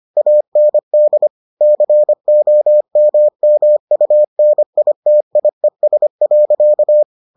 Morse code, heard as a keyed tone, about 25 wpm, 600 Hz. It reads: AND COMMUNITIES.